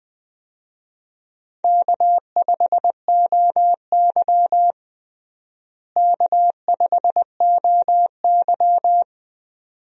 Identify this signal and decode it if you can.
Morse code, heard as a keyed tone, about 20 wpm, 700 Hz. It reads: K5OY K5OY
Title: USA Call Signs - 4 Characters - Repeated 2x - 20wpm